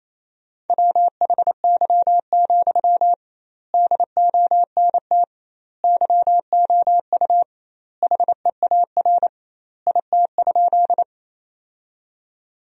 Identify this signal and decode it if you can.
Morse code, heard as a keyed tone, about 28 wpm, 700 Hz. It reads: WHY, DONT YOU HEAR IT?
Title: 2x Word-Spacing - Sentences from Top 300 Words - 28wpm